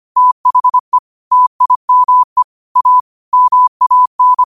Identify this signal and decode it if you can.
Morse code, heard as a keyed tone, about 25 wpm, 1000 Hz. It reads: THE TIME A MAN